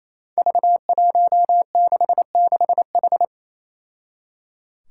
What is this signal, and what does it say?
Morse code, heard as a keyed tone, about 28 wpm, 700 Hz. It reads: V166H
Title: Five Challenging Letter-Number Combinations 28wpm